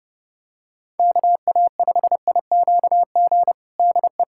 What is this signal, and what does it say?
Morse code, heard as a keyed tone, about 30 wpm, 700 Hz. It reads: KA5IQG DE
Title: USA Callsign DE Callsign - 6 Characters - 30wpm